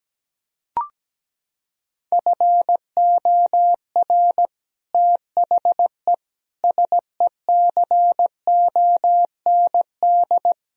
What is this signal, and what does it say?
Morse code, heard as a keyed tone, about 17 wpm, 700 Hz. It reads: FOR THE SECOND
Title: Sentences from Top 1000 Words - Review Part 1 - 17wpm